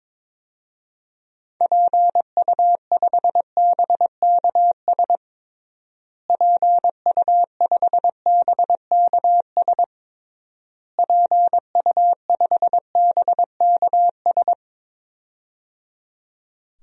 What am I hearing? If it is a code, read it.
Morse code, heard as a keyed tone, about 22 wpm, 700 Hz. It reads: PU5BKS PU5BKS PU5BKS
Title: DXCC-100 Call Signs - 6 Characters - Repeated 3x - 22wpm